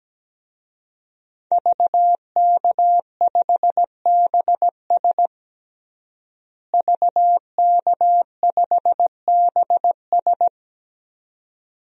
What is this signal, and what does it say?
Morse code, heard as a keyed tone, about 17 wpm, 700 Hz. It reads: VK5BS VK5BS
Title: DXCC-100 Call Signs - 5 Characters - Repeated 2x - 17wpm